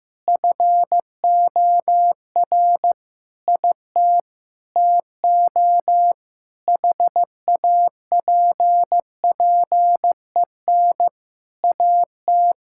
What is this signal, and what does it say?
Morse code, heard as a keyed tone, about 16 wpm, 700 Hz. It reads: FOR IT TO HAPPEN AT
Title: Sentences from Top 1000 Words - Review Part 6 - 15wpm